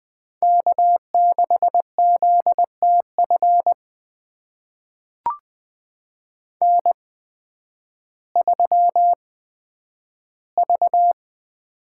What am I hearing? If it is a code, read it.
Morse code, heard as a keyed tone, about 20 wpm, spaced as for 11 wpm, 700 Hz. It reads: K6ZTF N3V
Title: USA Call Signs - 5 Characters - 8x Character Spacing - 20wpm